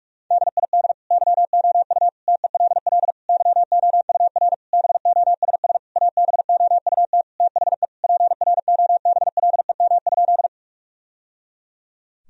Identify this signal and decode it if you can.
Morse code, heard as a keyed tone, about 46 wpm, 700 Hz. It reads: DID YOU TELL YOUR BOSS ABOUT THE PROBLEM?
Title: Sentences from Top 1000 Words - Review Part 4 - 45wpm